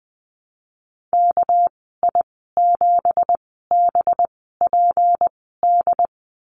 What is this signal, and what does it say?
Morse code, heard as a keyed tone, about 20 wpm, spaced as for 15 wpm, 700 Hz. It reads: KI7BPD
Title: 207 Lesson 33g - Letter j - Call Signs 20wpm